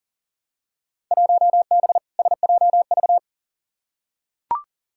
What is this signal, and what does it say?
Morse code, heard as a keyed tone, about 40 wpm, 700 Hz. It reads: 1B SJV
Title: Field Day Exchanges Encore - 40wpm